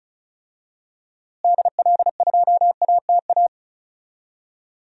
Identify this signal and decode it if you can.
Morse code, heard as a keyed tone, about 36 wpm, 700 Hz. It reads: DL2ATA
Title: DXCC-100 Call Signs - 6 Characters - Repeated 2x - 35wpm